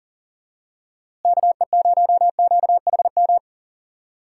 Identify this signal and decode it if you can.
Morse code, heard as a keyed tone, about 40 wpm, 700 Hz. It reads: KE0QHM
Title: USA Call Signs - 6 Characters - Repeated 2x - 40wpm